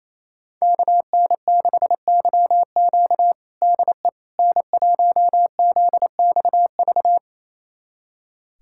Morse code, heard as a keyed tone, about 27 wpm, 700 Hz. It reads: KN6YQ DE N1ZXV
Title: USA Callsign DE Callsign - 5 Characters - 28wpm